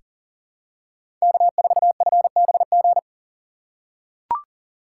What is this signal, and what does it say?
Morse code, heard as a keyed tone, about 40 wpm, 700 Hz. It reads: K4FBG
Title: USA Call Signs - 5 Characters - 3x Character Spacing - 40wpm